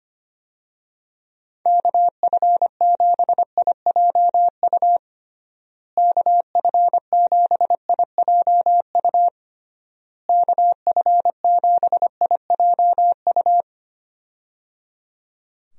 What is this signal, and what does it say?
Morse code, heard as a keyed tone, about 25 wpm, 700 Hz. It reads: KF7IJU KF7IJU KF7IJU
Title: USA Call Signs - 6 Characters - Repeated 3x - 25wpm